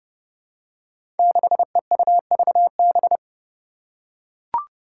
Morse code, heard as a keyed tone, about 32 wpm, 700 Hz. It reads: 6EUVB
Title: Five Challenging Letter-Number Combinations 30wpm